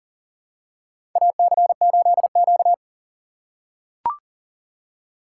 Morse code, heard as a keyed tone, about 40 wpm, 700 Hz. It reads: AC8Q
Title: USA Call Signs - 4x Character Spacing - 40wpm